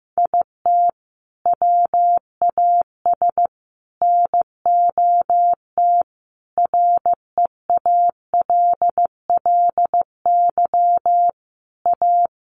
Morse code, heard as a keyed tone, about 15 wpm, 700 Hz. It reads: IT WAS NOT REALLY A
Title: Sentences from Top 600 Words 15wpm